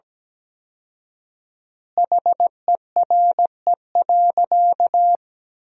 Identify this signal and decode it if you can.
Morse code, heard as a keyed tone, about 17 wpm, 700 Hz. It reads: HERE.
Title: 4x Word-Spacing - Sentences from Top 200 Words - 17wpm